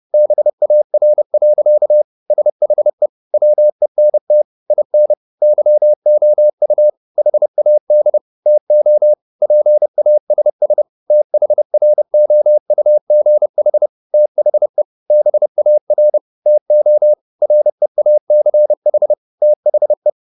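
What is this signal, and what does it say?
Morse code, heard as a keyed tone, about 30 wpm, 600 Hz. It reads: BAR. SHE WENT IN YOU HAD TO PASS THROUGH THE BAR TO REACH THE
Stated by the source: Morse code at 30 wpm, the novel Never by Ken Follett in English, from CW med Gnister.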